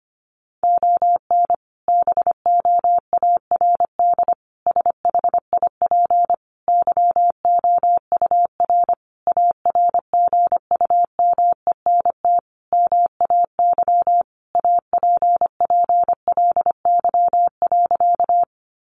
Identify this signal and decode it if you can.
Morse code, heard as a keyed tone, about 25 wpm, 700 Hz. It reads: ON BOARD SHIP YOUR ARGUMENT MAY APPLY.
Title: Sentences from Top 1400 Words 25wpm